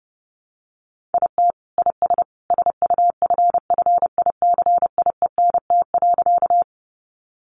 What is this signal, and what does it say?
Morse code, heard as a keyed tone, about 30 wpm, 700 Hz. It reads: IT IS SUFFICIENT.
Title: Sentences from Top 1400 Words 30wpm